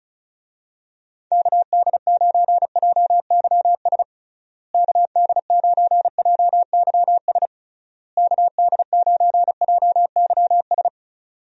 Morse code, heard as a keyed tone, about 34 wpm, 700 Hz. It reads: KD9JYS KD9JYS KD9JYS
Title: USA Call Signs - 6 Characters - Repeated 3x - 35wpm